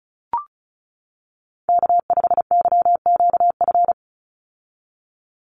Morse code, heard as a keyed tone, about 36 wpm, 700 Hz. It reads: K5YQF
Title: Call Signs - 35wpm